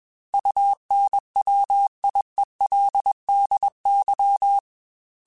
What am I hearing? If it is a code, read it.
Morse code, heard as a keyed tone, about 21 wpm, 800 Hz. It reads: UNWIELDY
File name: Technical Subjects / Technical Subjects-mp3s / AND Gate @20